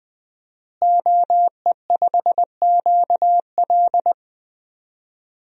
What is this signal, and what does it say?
Morse code, heard as a keyed tone, about 20 wpm, 700 Hz. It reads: OE5QL
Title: DXCC-100 Call Signs - 5 Characters - Repeated 2x - 20wpm